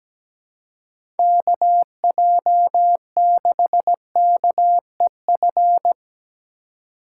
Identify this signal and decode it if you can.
Morse code, heard as a keyed tone, about 17 wpm, 700 Hz. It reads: KJ6KEF
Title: USA Call Signs - 6 Characters - 3x Character Spacing - 17wpm